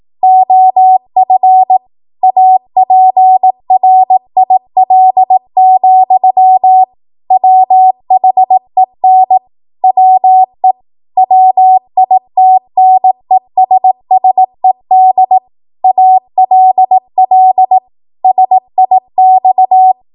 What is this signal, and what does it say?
Morse code, heard as a keyed tone, about 18 wpm, 750 Hz. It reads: OF APRIL, WHEN WE WITNESSED ALL SIX
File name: Morse / 18WPM